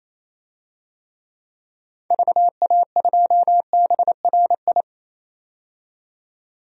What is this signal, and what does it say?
Morse code, heard as a keyed tone, about 28 wpm, 700 Hz. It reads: VA2BRI
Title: DXCC-100 Call Signs - 6 Characters - 28wpm